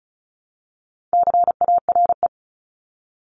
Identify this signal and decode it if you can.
Morse code, heard as a keyed tone, about 36 wpm, 700 Hz. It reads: CARE
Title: Multi-letter Phonemes 35wpm